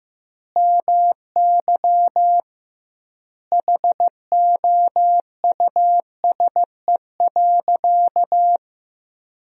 Morse code, heard as a keyed tone, about 15 wpm, 700 Hz. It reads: MY HOUSE.
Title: 2x Word-Spacing - Sentences from Top 200 Words - 15wpm